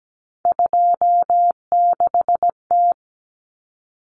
Morse code, heard as a keyed tone, about 17 wpm, 690 Hz. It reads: 26T
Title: Common Cut Numbers - 3-Digit 17wpm